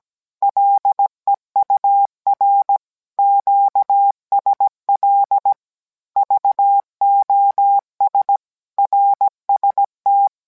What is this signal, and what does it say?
Morse code, heard as a keyed tone, about 17 wpm, 800 Hz. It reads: LEUR QSL VOS RST